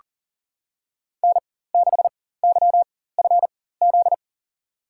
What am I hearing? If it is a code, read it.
Morse code, heard as a keyed tone, about 39 wpm, spaced as for 18 wpm, 700 Hz. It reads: N6YFZ
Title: USA Call Signs - 5 Characters - 4x Character Spacing - 40wpm